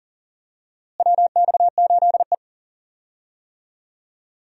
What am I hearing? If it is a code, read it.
Morse code, heard as a keyed tone, about 40 wpm, 700 Hz. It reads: WX8E
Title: USA Call Signs - 4 Characters - Repeated 2x - 40wpm